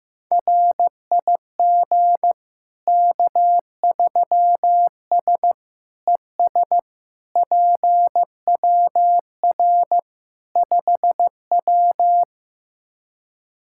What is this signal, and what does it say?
Morse code, heard as a keyed tone, about 15 wpm, 700 Hz. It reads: RIG K3S ES PWR 5W
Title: K1USN SST QSO Practice 15wpm